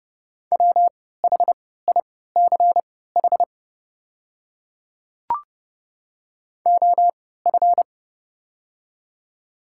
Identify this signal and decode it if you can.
Morse code, heard as a keyed tone, about 30 wpm, spaced as for 17 wpm, 700 Hz. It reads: WHICH OF
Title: 3x Character Spacing - Top 100 Words - 30wpm